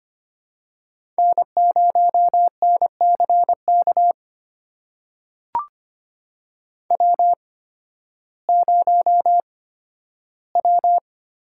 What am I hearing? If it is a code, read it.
Morse code, heard as a keyed tone, about 25 wpm, spaced as for 13 wpm, 700 Hz. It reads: N0NCK W0W
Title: USA Call Signs - 5 Characters - 8x Character Spacing - 25wpm